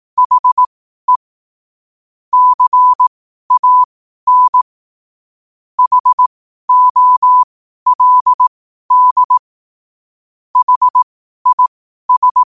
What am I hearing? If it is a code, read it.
Morse code, heard as a keyed tone, about 18 wpm, spaced as for 12 wpm, 1000 Hz. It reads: HE CAN HOLD HIS
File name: qotdfiles / 12wpm